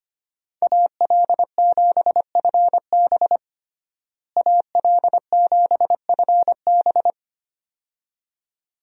Morse code, heard as a keyed tone, about 25 wpm, 700 Hz. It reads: AL7FB AL7FB
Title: USA Call Signs - 5 Characters - Repeated 2x - 25wpm